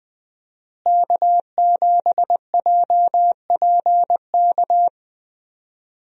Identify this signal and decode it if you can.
Morse code, heard as a keyed tone, about 20 wpm, 700 Hz. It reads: K7JPK
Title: USA Call Signs - 5 Characters - Repeated 2x - 20wpm